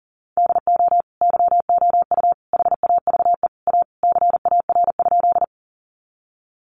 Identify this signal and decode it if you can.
Morse code, heard as a keyed tone, about 40 wpm, 700 Hz. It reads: DO YOU HAVE A CAR?
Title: Sentences from Top 300 Words 40wpm